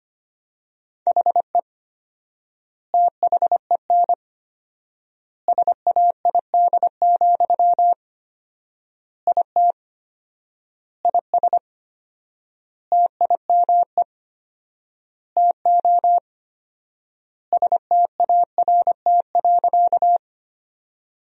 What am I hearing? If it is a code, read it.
Morse code, heard as a keyed tone, about 25 wpm, 700 Hz. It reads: HE THEN SAID, IT IS TIME TO START.
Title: 4x Word-Spacing - Sentences from Top 300 Words - 25wpm